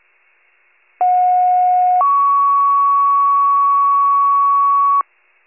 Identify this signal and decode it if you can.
Two-tone page, heard as a one-second tone, then a three-second tone, about 730 then 1120 Hz. It is heard from 1.0 to 5.0 s.
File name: wcalert